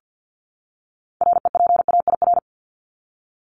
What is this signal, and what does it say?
Morse code, heard as a keyed tone, about 48 wpm, 700 Hz. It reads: REPAIR
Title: Top 1501-1750 Words 50wpm